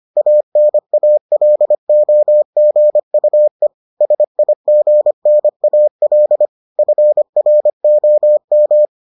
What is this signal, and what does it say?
Morse code, heard as a keyed tone, about 25 wpm, 600 Hz. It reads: ANALOGUE SIGNAL FROM